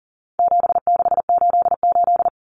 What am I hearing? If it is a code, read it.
Morse code, heard as a keyed tone, about 40 wpm, 700 Hz. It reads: 7688
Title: Common Cut Numbers - 4-Digit 40wpm